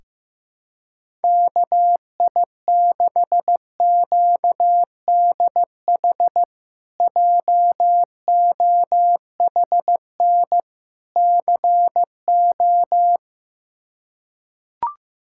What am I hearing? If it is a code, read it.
Morse code, heard as a keyed tone, about 15 wpm, 700 Hz. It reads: KI6QDH JOHN CO
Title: K1USN SST QSO Practice - 15wpm